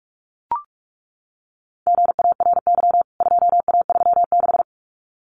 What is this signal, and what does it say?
Morse code, heard as a keyed tone, about 44 wpm, 700 Hz. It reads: GARY 2A36
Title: CWT Exchange 45wpm